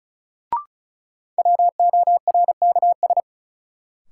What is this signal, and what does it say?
Morse code, heard as a keyed tone, about 36 wpm, 700 Hz. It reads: WORKS
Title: Sentences from Top 1000 Words - Review Part 5 - New Word Drill - 35wpm